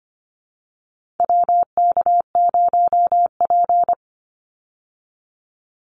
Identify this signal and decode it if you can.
Morse code, heard as a keyed tone, about 25 wpm, 700 Hz. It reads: WK0P
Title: Call Signs Repeated 2x - 25wpm